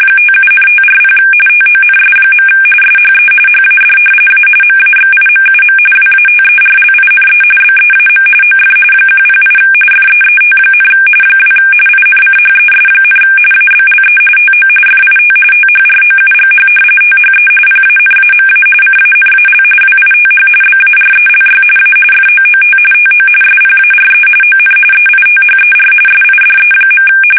RTTY, about 45.45 baud. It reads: 57,540 TONS. SHE HAD A MAXIMUM WATERLINE BEAM OF 108 FEET, 2 INCHES, ABLE TO FIT THROUGH THE PANAMA CANAL WITH BARELY A FOOT TO SPARE ON EITHER SIDE.
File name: fsk850 / battleshipnj